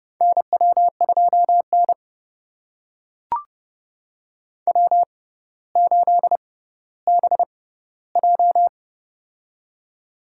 Morse code, heard as a keyed tone, about 32 wpm, spaced as for 16 wpm, 700 Hz. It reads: NW2N W8BJ
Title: USA Call Signs - 4 Characters - 6x Character Spacing - 30wpm